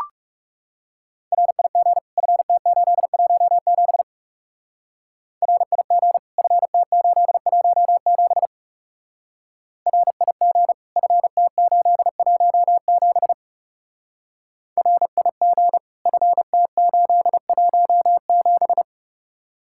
Morse code, heard as a keyed tone, about 36 wpm, 700 Hz. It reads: RIG FT817 RIG FT817 RIG FT817 RIG FT817
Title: Speed-Racing Common QSO Phrases - 30wpm